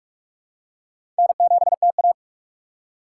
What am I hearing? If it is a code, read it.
Morse code, heard as a keyed tone, about 46 wpm, 700 Hz. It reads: N7TA
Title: USA Call Signs - 2x Character Spacing - 45wpm